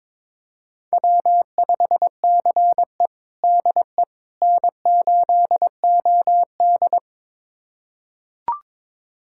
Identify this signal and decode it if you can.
Morse code, heard as a keyed tone, about 22 wpm, 700 Hz. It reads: W5CE DE N8OD
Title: USA Callsign DE Callsign - 4 Characters - 22wpm